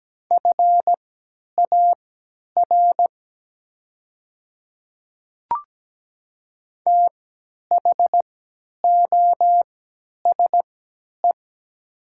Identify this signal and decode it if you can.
Morse code, heard as a keyed tone, about 17 wpm, spaced as for 10 wpm, 700 Hz. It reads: FAR THOSE
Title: 3x Character Spacing - Top 300 Words - 17wpm